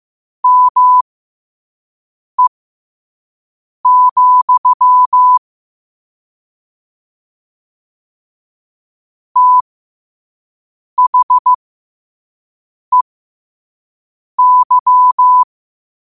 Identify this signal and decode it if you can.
Morse code, heard as a keyed tone, about 15 wpm, spaced as for 5 wpm, 1000 Hz. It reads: ME, THEY
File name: qotdfiles / 5wpm